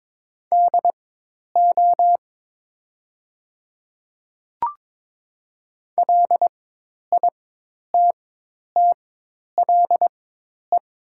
Morse code, heard as a keyed tone, about 22 wpm, spaced as for 10 wpm, 700 Hz. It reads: DO LITTLE
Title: 4x Character Spacing - Top 300 Words - 22wpm